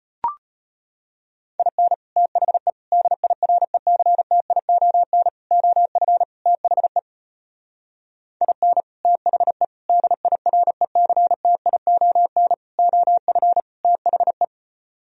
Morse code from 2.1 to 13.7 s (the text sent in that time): THE DIRECTION OF THE IN THE DIRECTION OF